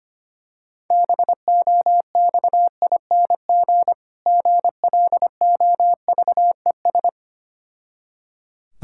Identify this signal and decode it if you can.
Morse code, heard as a keyed tone, about 25 wpm, 700 Hz. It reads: BOXING GLOVES
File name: practice / 25wpm / Difficult Words with Hints - 25wpm